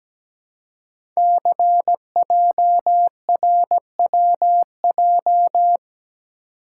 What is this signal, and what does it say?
Morse code, heard as a keyed tone, about 17 wpm, 700 Hz. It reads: CJRWJ
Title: Code Groups - 5 Letters - 17wpm